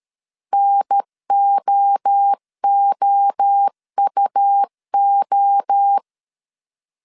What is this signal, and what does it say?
Morse code, heard as a keyed tone, about 13 wpm, 800 Hz. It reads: NOOUO
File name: CW / DET / 1 3 1500C S C 12wpm 2750